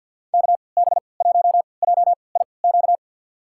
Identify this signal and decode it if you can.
Morse code, heard as a keyed tone, about 53 wpm, spaced as for 29 wpm, 700 Hz. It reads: KB1JIQ
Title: USA Call Signs - 6 Characters - 3x Character Spacing - 50wpm